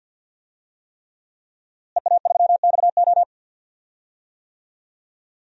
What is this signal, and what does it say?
Morse code, heard as a keyed tone, about 52 wpm, 700 Hz. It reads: EA3XO
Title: DXCC-100 Call Signs - 5 Characters - Repeated 3x - 50wpm